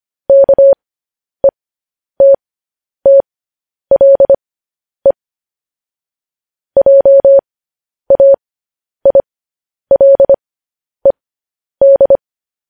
Morse code, heard as a keyed tone, about 25 wpm, spaced as for 10 wpm, 550 Hz. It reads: KETTLE JAILED
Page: news-headlines-in-morse-code-at-10-wpm